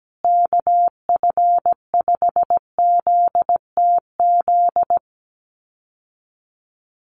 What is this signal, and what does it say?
Morse code, heard as a keyed tone, about 17 wpm, 700 Hz. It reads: KF5ZTZ
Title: USA Call Signs Repeated 3x 17wpm